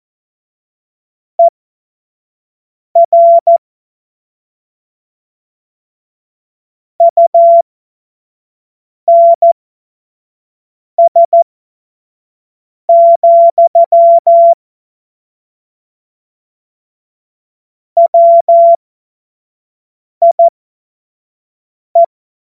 Morse code, heard as a keyed tone, about 14 wpm, spaced as for 5 wpm, 680 Hz. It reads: ER UNS, WIE
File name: Morsekurs / Witze / Witz 09. Februar 2021, 25 F70